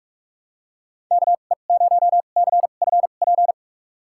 Morse code, heard as a keyed tone, about 43 wpm, spaced as for 32 wpm, 700 Hz. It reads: KE0CFP
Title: USA Call Signs - 2x Character Spacing - 45wpm